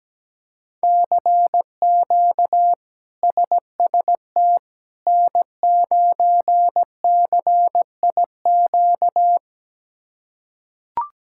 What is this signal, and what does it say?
Morse code, heard as a keyed tone, about 17 wpm, 700 Hz. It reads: CQ SST N9CIQ
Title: K1USN SST QSO Practice - 17wpm